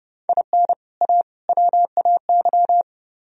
Morse code, heard as a keyed tone, about 30 wpm, 700 Hz. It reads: IN A WAY